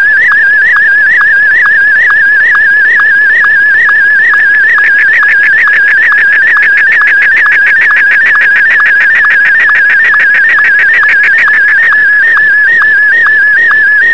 SSTV (slow-scan television), Martin 1.